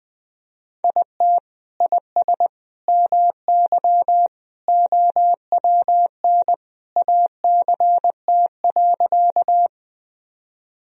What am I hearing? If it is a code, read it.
Morse code, heard as a keyed tone, about 21 wpm, 700 Hz. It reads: IT IS MY OWN ACT.
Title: Sentences from Top 1000 Words - Review Part 1 - 20wpm